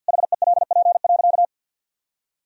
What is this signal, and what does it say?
Morse code, heard as a keyed tone, about 50 wpm, 700 Hz. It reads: HELP.
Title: Sentences from Top 1000 Words - Review Part 4 - 50wpm